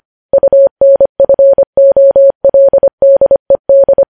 Morse code, heard as a keyed tone, about 25 wpm, 550 Hz. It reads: UNFOLDED
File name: podcast / News headlines-1x-25x25